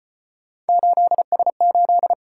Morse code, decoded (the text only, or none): 8S8